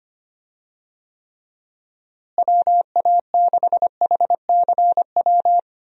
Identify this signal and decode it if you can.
Morse code, heard as a keyed tone, about 25 wpm, 700 Hz. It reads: WA6HCW
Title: USA Call Signs - 6 Characters - Repeated 2x - 25wpm